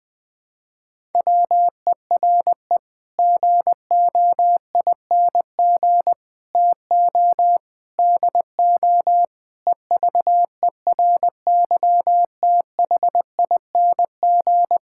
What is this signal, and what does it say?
Morse code, heard as a keyed tone, about 20 wpm, 700 Hz. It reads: WERE GOING TO DO EVERYTHING
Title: Sets of 8 Words - 20wpm